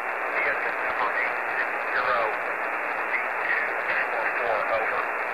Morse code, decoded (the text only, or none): none